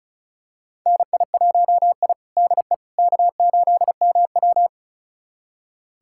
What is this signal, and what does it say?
Morse code, heard as a keyed tone, about 36 wpm, 700 Hz. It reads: NI1I DE K8MW
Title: USA Callsign DE Callsign - 4 Characters - 35wpm